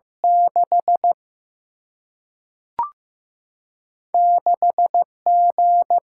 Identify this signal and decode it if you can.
Morse code, heard as a keyed tone, about 15 wpm, 700 Hz. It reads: 6 6 G
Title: Single-Letter Single-Number 15wpm